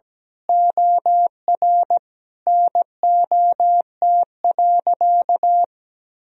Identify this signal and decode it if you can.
Morse code, heard as a keyed tone, about 17 wpm, 700 Hz. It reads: OR NOT.
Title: Sentences from Top 1000 Words - Review Part 4 - 17wpm